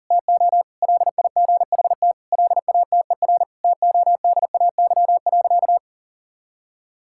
Morse code, heard as a keyed tone, about 40 wpm, 700 Hz. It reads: TO LIGHT LATER TODAY.
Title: Sentences from Top 1000 Words - Review Part 6 - 40wpm